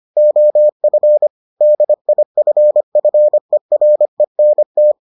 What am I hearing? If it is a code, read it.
Morse code, heard as a keyed tone, about 25 wpm, 600 Hz. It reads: OF DIFFERENT